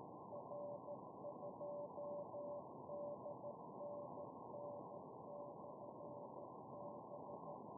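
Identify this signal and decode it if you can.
Morse code, heard as a keyed tone, about 14 wpm, 580 Hz, with noise. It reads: R2DNN/B
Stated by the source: 120 mW QRP beacon UI7K, heard with QSB fading to zero in Kirov oblast and recorded on an SDR; it sends temperature and supply-voltage digits.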